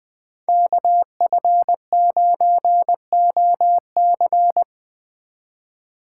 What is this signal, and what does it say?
Morse code, heard as a keyed tone, about 20 wpm, 700 Hz. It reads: KF9OC
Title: Speed Racing - USA 2x2 Call Signs 17wpm